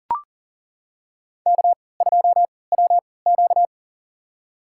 Morse code, heard as a keyed tone, about 40 wpm, spaced as for 23 wpm, 700 Hz. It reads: K2WQ
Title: USA Call Signs - 4 Characters - 3x Character Spacing - 40wpm